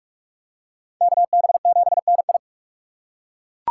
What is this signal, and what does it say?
Morse code, heard as a keyed tone, about 41 wpm, 700 Hz. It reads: KB7NI